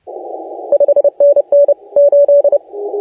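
Morse code, heard as a keyed tone, about 28 wpm, 570 Hz. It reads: 5NN 8
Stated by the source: Morse contest audio recorded by LA4C during the SAC CW 2018 contest.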